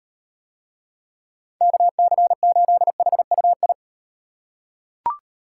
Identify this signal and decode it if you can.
Morse code, heard as a keyed tone, about 38 wpm, 700 Hz. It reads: KC8HUI